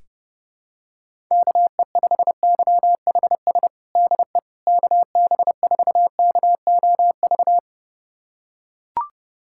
Morse code, heard as a keyed tone, about 30 wpm, 700 Hz. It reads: KE5YHS DE KB4KOV
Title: USA Callsign DE Callsign - 6 Characters - 30wpm